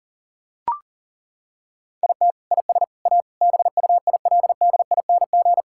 Morse code, heard as a keyed tone, about 40 wpm, 700 Hz. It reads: IT IS A BUILDING
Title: Sentences from Top 1000 Words - Review Part 5 - 40wpm